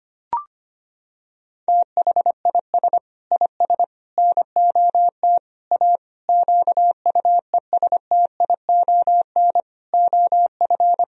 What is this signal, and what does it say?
Morse code, heard as a keyed tone, about 25 wpm, 700 Hz. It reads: THIS IS NOT A QUESTION OF
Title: Sentences from Top 1000 Words - Review Part 2 - 25wpm